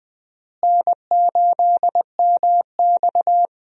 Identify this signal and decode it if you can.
Morse code, heard as a keyed tone, about 21 wpm, 700 Hz. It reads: N8MX
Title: USA Call Signs - 4 Characters - Repeated 2x - 20wpm